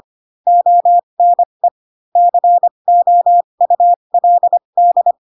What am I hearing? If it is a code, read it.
Morse code, heard as a keyed tone, about 25 wpm, spaced as for 22 wpm, 700 Hz. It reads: ONE COULD
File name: Pinocchio / Pinocchio ch2 18 25 ik1hge